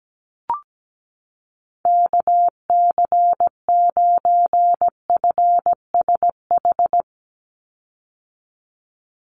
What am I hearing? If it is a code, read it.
Morse code, heard as a keyed tone, about 17 wpm, 700 Hz. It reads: KC9FSH
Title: USA Call Signs 17wpm